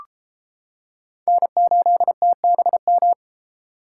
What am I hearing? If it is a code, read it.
Morse code, heard as a keyed tone, about 30 wpm, 700 Hz. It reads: N8TBM